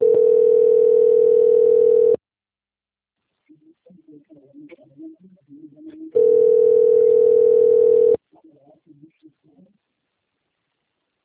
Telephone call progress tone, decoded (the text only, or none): ringback tone